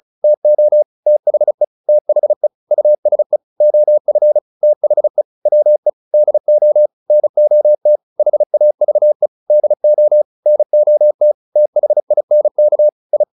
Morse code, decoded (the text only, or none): TO THE THE USE OF THE WE DO NOT HAVE DO NOT THINK I